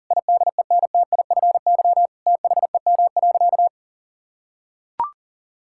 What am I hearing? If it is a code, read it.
Morse code, heard as a keyed tone, about 40 wpm, 700 Hz. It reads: IDENTIFY THEM.
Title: Sentences from Top 1000 Words - Review Part 6 - 40wpm